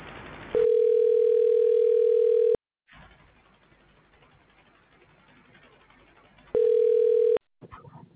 Ringback tone (North America), 440 + 480 Hz.